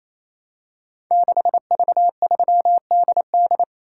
Morse code, heard as a keyed tone, about 27 wpm, 700 Hz. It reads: 6V3DD
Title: Code Groups - 5 Letter:Numbers - 28wpm